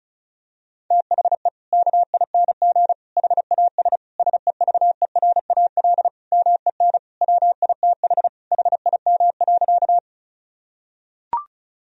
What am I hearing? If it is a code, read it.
Morse code, heard as a keyed tone, about 34 wpm, 700 Hz. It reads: THE KING HAS SEVERAL MEN WITH HIM.
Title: Sentences from Top 1000 Words - Review Part 2 - 35wpm